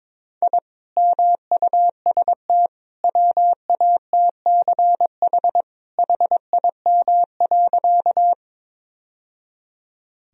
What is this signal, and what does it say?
Morse code, heard as a keyed tone, about 22 wpm, 700 Hz. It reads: I MUST WATCH HIM.
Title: Sentences from Top 1000 Words - Review Part 1 - 22wpm